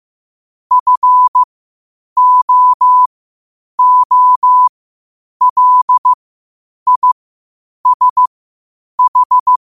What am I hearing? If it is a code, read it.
Morse code, heard as a keyed tone, about 15 wpm, spaced as for 8 wpm, 1000 Hz. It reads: FOOLISH
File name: qotdfiles / 8wpm